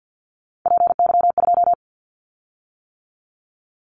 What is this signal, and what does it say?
Morse code, heard as a keyed tone, about 50 wpm, 700 Hz. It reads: PY2
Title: Callsign Prefixes for Top 50 DXCC - 50wpm